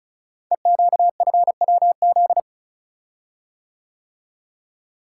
Morse code, heard as a keyed tone, about 36 wpm, 700 Hz. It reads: EQFWZ
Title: Code Groups - 5 Letter:Numbers - 35wpm